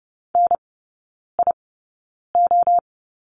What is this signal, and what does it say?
Morse code, heard as a keyed tone, about 30 wpm, spaced as for 9 wpm, 700 Hz. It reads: NIO
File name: course / 30:10wpm / 087 Lesson 15e - Number 5 - Three Characters 30wpm